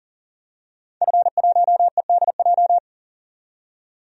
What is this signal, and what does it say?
Morse code, heard as a keyed tone, about 37 wpm, 700 Hz. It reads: F1EDJ
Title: DXCC-100 Call Signs - 5 Characters - 40wpm